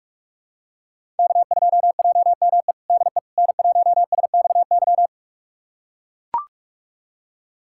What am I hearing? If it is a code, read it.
Morse code, heard as a keyed tone, about 44 wpm, 700 Hz. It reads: K2JME DE N1SXY
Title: USA Callsign DE Callsign - 5 Characters - 45wpm